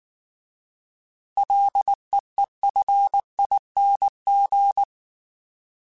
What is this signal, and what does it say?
Morse code, heard as a keyed tone, about 19 wpm, 800 Hz. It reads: LEEFING